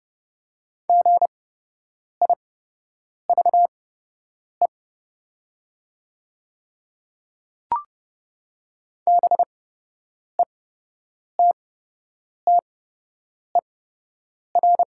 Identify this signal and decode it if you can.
Morse code, heard as a keyed tone, about 30 wpm, spaced as for 8 wpm, 700 Hz. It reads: GIVE BETTER
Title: 8x Character Spacing - Top 500 Words - 30wpm